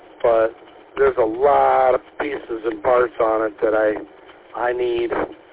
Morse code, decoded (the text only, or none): none